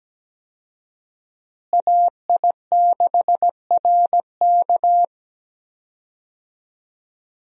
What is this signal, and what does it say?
Morse code, heard as a keyed tone, about 17 wpm, 700 Hz. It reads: AI6RK
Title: USA Call Signs - 5 Characters - Repeated 3x - 17wpm